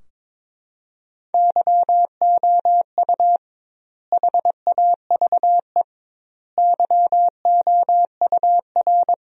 Morse code, heard as a keyed tone, about 22 wpm, 700 Hz. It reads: YOU HAVE YOUR
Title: 2x Word-Spacing - Sentences from Top 400 Words - 22wpm